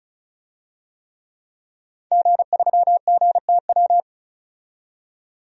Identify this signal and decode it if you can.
Morse code, heard as a keyed tone, about 34 wpm, 700 Hz. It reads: G3GTW
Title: DXCC-100 Call Signs - 5 Characters - 35wpm